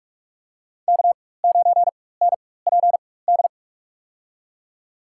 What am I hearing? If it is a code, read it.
Morse code, heard as a keyed tone, about 46 wpm, spaced as for 21 wpm, 700 Hz. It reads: K9NPD
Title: USA Call Signs - 5 Characters - 4x Character Spacing - 45wpm